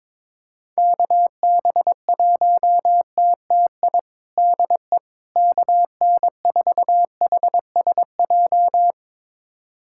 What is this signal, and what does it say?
Morse code, heard as a keyed tone, about 22 wpm, 700 Hz. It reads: KB1TTI DE KN4HSJ
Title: USA Callsign DE Callsign 22wpm